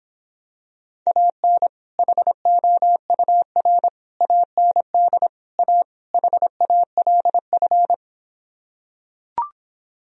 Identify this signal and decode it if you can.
Morse code, heard as a keyed tone, about 26 wpm, 700 Hz. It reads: AN HOUR AND A HALF